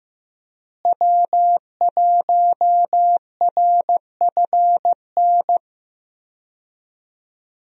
Morse code, heard as a keyed tone, about 15 wpm, 700 Hz. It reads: W1RFN
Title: USA Call Signs - 5 Characters - Repeated 3x - 15wpm